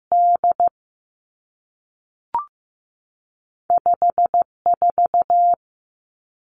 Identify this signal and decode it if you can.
Morse code, heard as a keyed tone, about 15 wpm, 700 Hz. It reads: D 5 4
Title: Two Challenging Letter-Number Combinations 15wpm